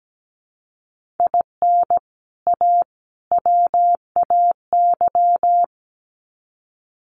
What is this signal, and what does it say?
Morse code, heard as a keyed tone, about 17 wpm, 700 Hz. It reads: IN A WAY
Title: Sets of 3 Words 17wpm